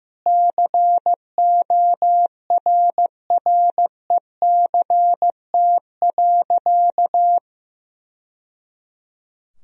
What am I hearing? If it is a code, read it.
Morse code, heard as a keyed tone, about 14 wpm, 700 Hz. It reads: CORRECT.
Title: Sentences from Top 1000 Words - Review Part 2 - 15wpm